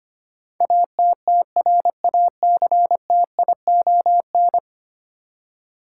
Morse code, heard as a keyed tone, about 25 wpm, 700 Hz. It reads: ATTRACTION
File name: Ten Letter Words - 25wpm